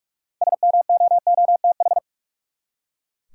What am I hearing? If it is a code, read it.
Morse code, heard as a keyed tone, about 44 wpm, 700 Hz. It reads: SMOOTH
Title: Six Letter Words - 45wpm